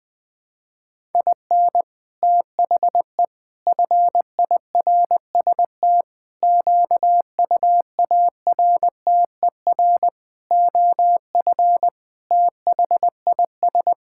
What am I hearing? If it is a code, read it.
Morse code, heard as a keyed tone, about 20 wpm, 700 Hz. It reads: IN THE FIRST QUARTER OF THIS
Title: Sentences with 7-Word Phrases - Spoken Beginning - 20wpm